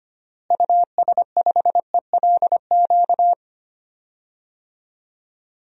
Morse code, heard as a keyed tone, about 25 wpm, 700 Hz. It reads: US5ELQ
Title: DXCC-100 Call Signs - 6 Characters - Repeated 2x - 25wpm